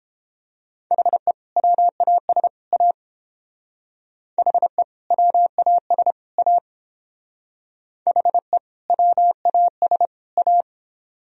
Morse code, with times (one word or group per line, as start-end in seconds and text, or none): 0.9-1.3 HE
1.6-2.5 WAS
2.7-2.9 A
4.4-4.8 HE
5.1-6.1 WAS
6.4-6.6 A
8.1-8.6 HE
8.9-10.1 WAS
10.4-10.6 A